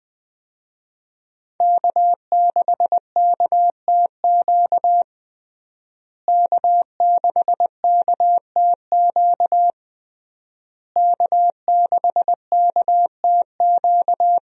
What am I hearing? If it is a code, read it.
Morse code, heard as a keyed tone, about 20 wpm, 700 Hz. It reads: K6KTQ K6KTQ K6KTQ
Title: USA Call Signs - 5 Characters - Repeated 3x - 20wpm